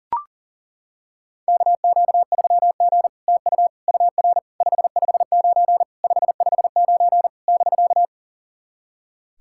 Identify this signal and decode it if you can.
Morse code, heard as a keyed tone, about 37 wpm, 700 Hz. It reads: KQ3G TU UR 559 559 <BK>
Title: SOTA QSO Practice 40wpm